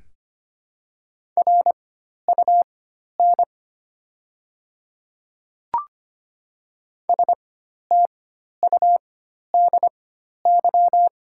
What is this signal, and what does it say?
Morse code, heard as a keyed tone, about 25 wpm, spaced as for 12 wpm, 700 Hz. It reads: RUN STUDY
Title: 4x Character Spacing - Top 300 Words - 25wpm